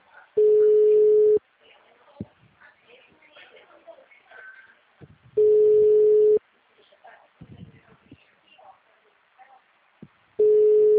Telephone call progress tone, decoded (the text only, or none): ringback tone